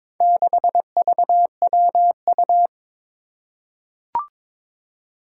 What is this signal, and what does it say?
Morse code, heard as a keyed tone, about 22 wpm, 700 Hz. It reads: 6VWU
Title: Four Challenging Letter-Number Combinations 22wpm